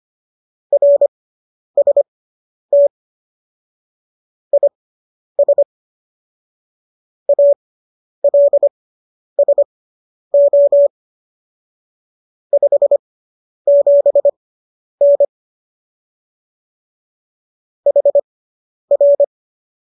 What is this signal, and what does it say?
Morse code, heard as a keyed tone, about 25 wpm, spaced as for 10 wpm, 570 Hz. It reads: RST IS ALSO 57N HR